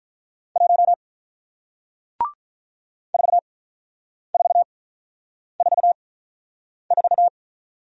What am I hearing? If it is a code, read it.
Morse code, heard as a keyed tone, about 48 wpm, spaced as for 9 wpm, 700 Hz. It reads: 1 4444